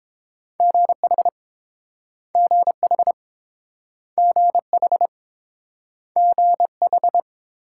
Morse code, heard as a keyed tone, about 30 wpm, 700 Hz. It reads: GH GH GH GH